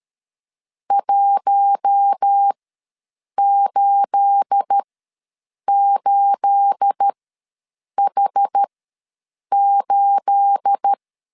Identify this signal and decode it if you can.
Morse code, heard as a keyed tone, about 13 wpm, spaced as for 7 wpm, 800 Hz. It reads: 188H8
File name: LEZIONI / 9 28 ws 12wpm 3354